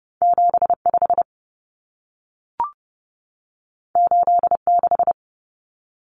Morse code, heard as a keyed tone, about 30 wpm, 700 Hz. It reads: 75 86
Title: Numbers - 2-Digit 30wpm